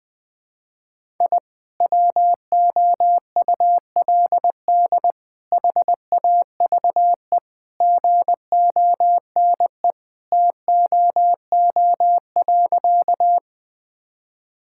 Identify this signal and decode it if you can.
Morse code, heard as a keyed tone, about 20 wpm, 700 Hz. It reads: I WOULD HAVE GONE TOO.
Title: Sentences with 2-Word Phrases - 20wpm